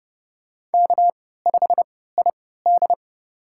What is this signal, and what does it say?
Morse code, heard as a keyed tone, about 30 wpm, spaced as for 17 wpm, 700 Hz. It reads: K5ID
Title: USA Call Signs - 4 Characters - 3x Character Spacing - 30wpm